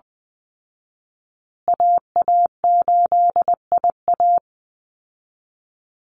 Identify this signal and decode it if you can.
Morse code, heard as a keyed tone, about 20 wpm, 700 Hz. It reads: AA8IA
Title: Call Signs Repeated 3x - 20wpm